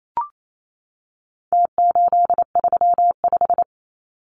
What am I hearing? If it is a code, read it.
Morse code, heard as a keyed tone, about 28 wpm, 700 Hz. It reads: T835
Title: Numbers - Contest Style - 4-Digit 28wpm